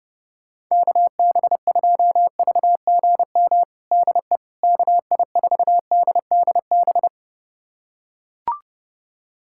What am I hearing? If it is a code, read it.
Morse code, heard as a keyed tone, about 30 wpm, 700 Hz. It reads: KB2VGM DE KI4DDB
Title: USA Callsign DE Callsign - 6 Characters - 30wpm